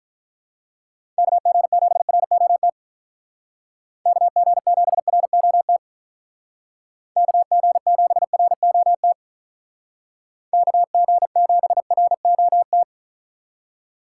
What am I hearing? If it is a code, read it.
Morse code, heard as a keyed tone, about 43 wpm, 700 Hz. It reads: KG7ROT KG7ROT KG7ROT KG7ROT
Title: Speed Racing - USA 2x3 Call Signs 35wpm